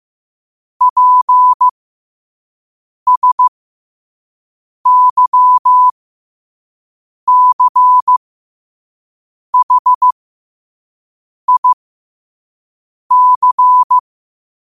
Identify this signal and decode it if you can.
Morse code, heard as a keyed tone, about 15 wpm, spaced as for 5 wpm, 1000 Hz. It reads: PSYCHIC
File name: qotdfiles / 5wpm